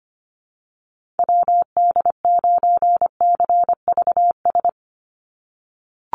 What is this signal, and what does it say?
Morse code, heard as a keyed tone, about 25 wpm, 700 Hz. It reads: WD9CVS
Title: USA Call Signs 25wpm